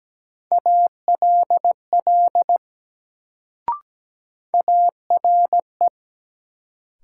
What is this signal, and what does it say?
Morse code, heard as a keyed tone, about 17 wpm, 700 Hz. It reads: ALL ARE
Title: Sentences from Top 100 Words - New Word Drill - 17wpm